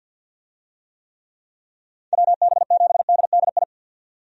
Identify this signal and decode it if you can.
Morse code, heard as a keyed tone, about 49 wpm, 700 Hz. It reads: WB7DDI
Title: USA Call Signs - 6 Characters - 50wpm